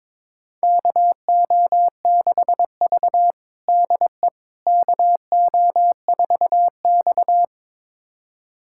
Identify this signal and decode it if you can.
Morse code, heard as a keyed tone, about 22 wpm, 700 Hz. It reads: KO6V DE KO4X
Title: USA Callsign DE Callsign - 4 Characters - 22wpm